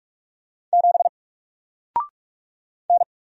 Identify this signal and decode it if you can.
Morse code, heard as a keyed tone, about 44 wpm, 700 Hz. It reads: 7 N
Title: Single Letter-Number Alt 45wpm